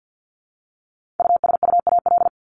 Morse code, heard as a keyed tone, about 48 wpm, 700 Hz. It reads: USUAL
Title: Top 500 Words 50wpm